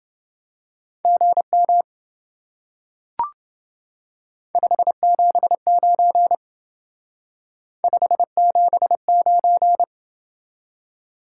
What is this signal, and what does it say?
Morse code, heard as a keyed tone, about 28 wpm, 700 Hz. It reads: GM 579 579